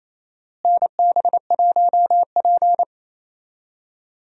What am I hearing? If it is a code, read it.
Morse code, heard as a keyed tone, about 28 wpm, 700 Hz. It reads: NB1P
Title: USA Call Signs - 4 Characters - Repeated 3x - 28wpm